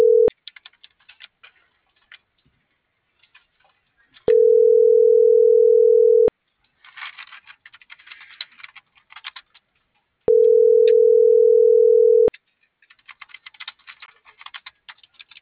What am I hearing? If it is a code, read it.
Ringback tone (North America), 440 + 480 Hz.